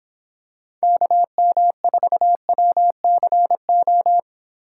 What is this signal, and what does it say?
Morse code, heard as a keyed tone, about 25 wpm, 700 Hz. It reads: KM4WCO